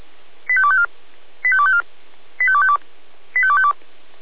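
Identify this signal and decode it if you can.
Five-tone selective call (ZVEI), heard as quick runs of five tones, tones of about 70 ms. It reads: 85254 85254 85252 85252
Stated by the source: ZVEI five-tone selective call received by SDR.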